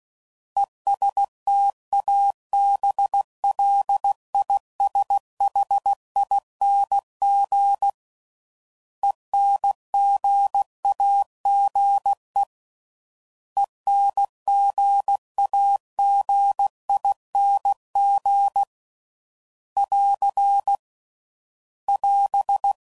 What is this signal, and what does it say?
Morse code, heard as a keyed tone, about 16 wpm, 800 Hz. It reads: ESTABLISHING ENGAGE ENGAGING <AR> <AS>